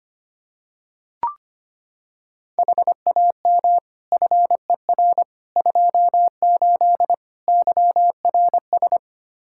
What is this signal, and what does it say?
Morse code, heard as a keyed tone, about 25 wpm, 700 Hz. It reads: HAM FER 28 YRS
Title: Practice QSO - Exchange 3 - More Info - 25wpm